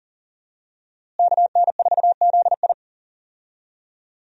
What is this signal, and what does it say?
Morse code, heard as a keyed tone, about 40 wpm, 700 Hz. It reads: KN4ZI
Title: USA Call Signs - 5 Characters - Repeated 2x - 40wpm